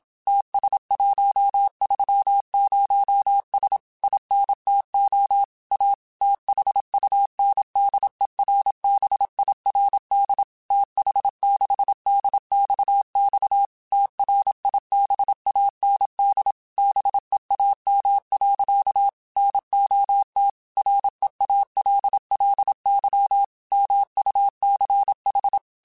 Morse code, heard as a keyed tone, about 27 wpm, 800 Hz. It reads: TS130S INTO A THUNDERBIRD TH6DXX TRIBAND BEAM. NOT REALLY MUCH
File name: QSOs / QSOs-mp3s / WD0BGP 1@25